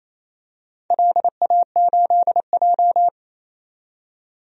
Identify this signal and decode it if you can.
Morse code, heard as a keyed tone, about 27 wpm, 700 Hz. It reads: LA8J
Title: DXCC-100 Call Signs - 4 Characters - 28wpm